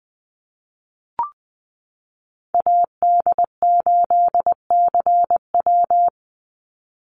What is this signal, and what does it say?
Morse code, heard as a keyed tone, about 20 wpm, 700 Hz. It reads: AD8CW
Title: Call Signs - 20wpm